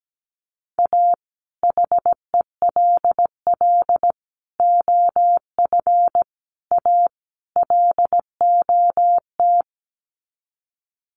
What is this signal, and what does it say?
Morse code, heard as a keyed tone, about 17 wpm, 700 Hz. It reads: A HELL OF A LOT
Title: Sets of 5 Words Encore 17wpm